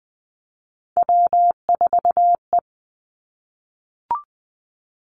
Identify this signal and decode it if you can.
Morse code, heard as a keyed tone, about 20 wpm, 700 Hz. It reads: W4E
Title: Three Challenging Letter-Number Combinations 20wpm